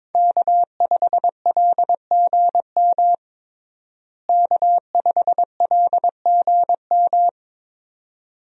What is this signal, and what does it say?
Morse code, heard as a keyed tone, about 22 wpm, 700 Hz. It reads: K5LGM K5LGM
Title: USA Call Signs - 5 Characters - Repeated 2x - 22wpm